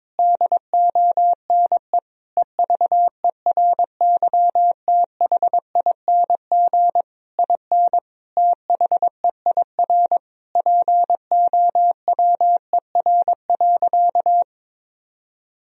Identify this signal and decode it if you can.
Morse code, heard as a keyed tone, about 22 wpm, 700 Hz. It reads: DONE EVERYTHING IN THEIR POWER.
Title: Sentences from Top 1000 Words - Review Part 2 - 22wpm